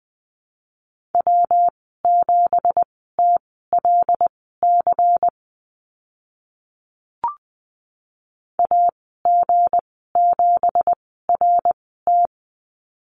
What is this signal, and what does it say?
Morse code, heard as a keyed tone, about 20 wpm, spaced as for 15 wpm, 700 Hz. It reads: W7TLC AG7RT
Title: 179 Lesson 29g - Slash - Call Signs 20wpm